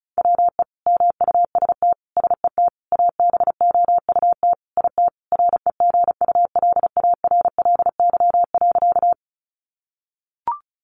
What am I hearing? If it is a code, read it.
Morse code, heard as a keyed tone, about 34 wpm, 700 Hz. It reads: WE MUST SET ABOUT IT REGULARLY.
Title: Sentences from Top 1400 Words 35wpm